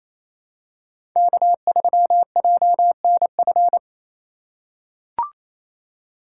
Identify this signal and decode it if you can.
Morse code, heard as a keyed tone, about 28 wpm, 700 Hz. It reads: K3JNF
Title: USA Call Signs - 4x Character Spacing - 28wpm